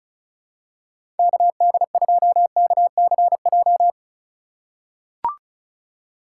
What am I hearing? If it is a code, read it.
Morse code, heard as a keyed tone, about 36 wpm, 700 Hz. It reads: KD2KCJ
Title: USA Call Signs - 6 Characters - 3x Character Spacing - 35wpm